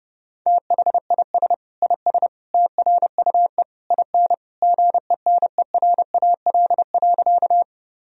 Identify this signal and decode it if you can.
Morse code, heard as a keyed tone, about 30 wpm, 700 Hz. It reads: THIS IS TRUE IN GENERAL.
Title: Sentences from Top 1000 Words - Review Part 3 - 30wpm